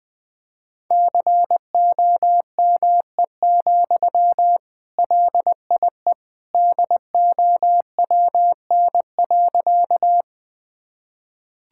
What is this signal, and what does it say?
Morse code, heard as a keyed tone, about 20 wpm, 700 Hz. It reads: COME, LIE DOWN.
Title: Sentences from Top 1600 Words 20wpm